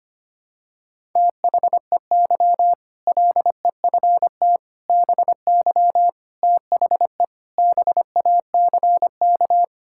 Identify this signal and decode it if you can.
Morse code, heard as a keyed tone, about 25 wpm, 700 Hz. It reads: THEY LEFT BY THE BACK
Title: Sentences from Top 1600 Words 25wpm